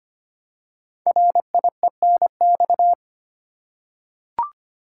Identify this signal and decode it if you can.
Morse code, heard as a keyed tone, about 25 wpm, 700 Hz. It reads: RIENX
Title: Code Groups - 5 Letters - 25wpm